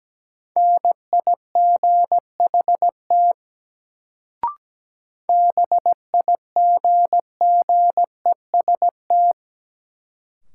Morse code, heard as a keyed tone, about 17 wpm, 700 Hz. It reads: NIGHT BIGGEST
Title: Sets of 4 Words - New Word Drill - 17wpm